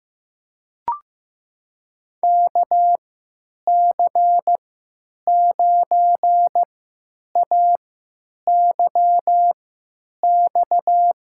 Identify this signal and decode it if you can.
Morse code, heard as a keyed tone, about 15 wpm, spaced as for 9 wpm, 700 Hz. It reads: KC9AYX
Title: USA Call Signs - 6 Characters - 3x Character Spacing - 15wpm